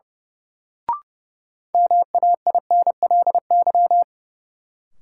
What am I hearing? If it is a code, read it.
Morse code, heard as a keyed tone, about 30 wpm, 700 Hz. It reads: MAINLY
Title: Sentences from Top 1400 Words - New Word Drill - 30wpm